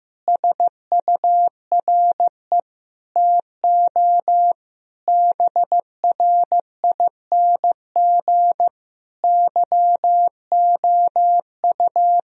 Morse code, heard as a keyed tone, about 16 wpm, 700 Hz. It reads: SURE TO BRING YOU
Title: Sentences from Top 1000 Words - Review Part 3 - 15wpm